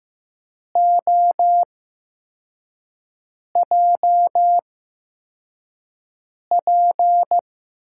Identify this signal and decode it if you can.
Morse code, heard as a keyed tone, about 15 wpm, 700 Hz. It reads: O J P
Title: USA Call Signs - 8x Character Spacing - 15wpm